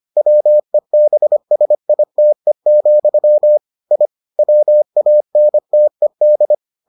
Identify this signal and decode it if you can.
Morse code, heard as a keyed tone, about 25 wpm, 600 Hz. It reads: WEBSITE, I WANTED